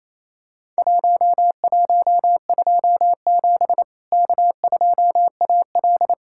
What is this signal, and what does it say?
Morse code, heard as a keyed tone, about 28 wpm, 700 Hz. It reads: 1127 K2AL
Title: Worked All Europe DX Contest - QTC Exchange - 28wpm